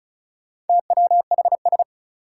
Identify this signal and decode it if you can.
Morse code, heard as a keyed tone, about 36 wpm, 700 Hz. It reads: TWHS
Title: Four Challenging Letter-Number Combinations 35wpm